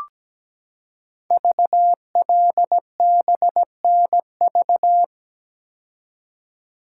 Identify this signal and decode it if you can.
Morse code, heard as a keyed tone, about 17 wpm, 700 Hz. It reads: VLBNV
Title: Code Groups - 5 Letter:Numbers - 17wpm